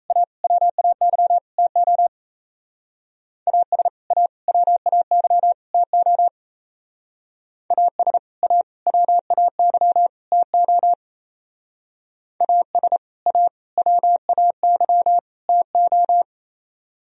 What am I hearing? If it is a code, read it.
Morse code, heard as a keyed tone, about 35 wpm, 700 Hz. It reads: A WAY TO AS A WAY TO AS A WAY TO AS A WAY TO
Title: Speed-Racing - Sets of 4 Words 28wpm